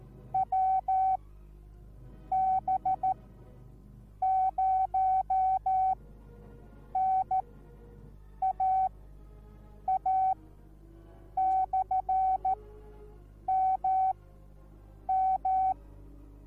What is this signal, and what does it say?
Morse code, heard as a keyed tone, about 14 wpm, spaced as for 6 wpm, 760 Hz. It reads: WB0NAA/MM